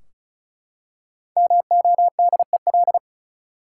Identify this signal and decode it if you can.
Morse code, heard as a keyed tone, about 34 wpm, 700 Hz. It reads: MODEL
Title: Five Letter Words - 35wpm